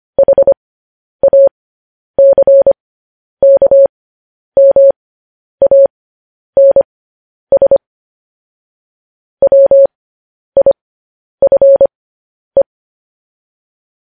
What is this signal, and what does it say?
Morse code, heard as a keyed tone, about 25 wpm, spaced as for 10 wpm, 550 Hz. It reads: HACKMANS WIFE
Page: news headlines in morse code at 10 wpm